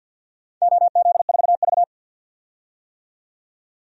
Morse code, heard as a keyed tone, about 51 wpm, 700 Hz. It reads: OZ4V
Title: DXCC-100 Call Signs - 4 Characters - 50wpm